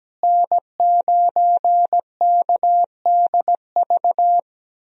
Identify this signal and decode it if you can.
Morse code, heard as a keyed tone, about 17 wpm, 700 Hz. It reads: N9KDV
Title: USA Call Signs - 5 Characters - Repeated 2x - 17wpm